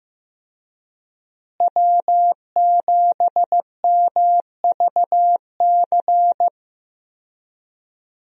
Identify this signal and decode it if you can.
Morse code, heard as a keyed tone, about 15 wpm, 700 Hz. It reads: W7MVC
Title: USA Call Signs - 5 Characters - Repeated 2x - 15wpm